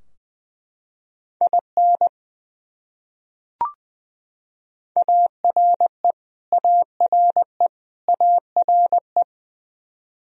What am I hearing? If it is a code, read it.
Morse code, heard as a keyed tone, about 20 wpm, 700 Hz. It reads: IN ARE ARE ARE
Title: Top 100 Words Repeated 20wpm